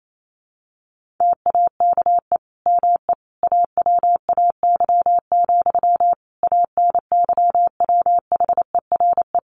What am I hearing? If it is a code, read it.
Morse code, heard as a keyed tone, about 28 wpm, 700 Hz. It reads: TAKE ME AWAY, ANYWHERE
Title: Sentences from Top 1400 Words 28wpm